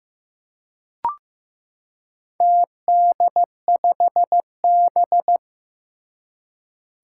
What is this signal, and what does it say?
Morse code, heard as a keyed tone, about 15 wpm, 700 Hz. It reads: TD5B
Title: Four Challenging Letter-Number Combinations 15wpm